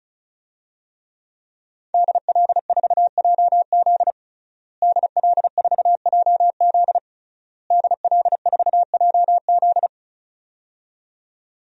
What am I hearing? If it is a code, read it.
Morse code, heard as a keyed tone, about 34 wpm, 700 Hz. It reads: DL4JZ DL4JZ DL4JZ
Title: DXCC-100 Call Signs - 5 Characters - Repeated 3x - 35wpm